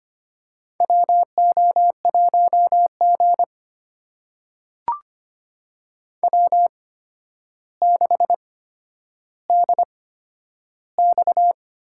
Morse code, heard as a keyed tone, about 25 wpm, spaced as for 11 wpm, 700 Hz. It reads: WO1G W6DX
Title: USA Call Signs - 4 Characters - 8x Character Spacing - 25wpm